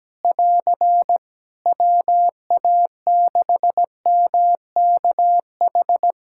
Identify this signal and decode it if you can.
Morse code, heard as a keyed tone, about 17 wpm, 700 Hz. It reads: <AR> WA6MKH
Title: Practice QSO - Exchange 4 - Ending - 17wpm